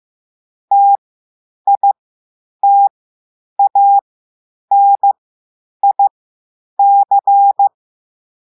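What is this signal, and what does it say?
Morse code, heard as a keyed tone, about 15 wpm, spaced as for 9 wpm, 800 Hz. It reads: TITANIC
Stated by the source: Slow Morse test message simulating a distress call.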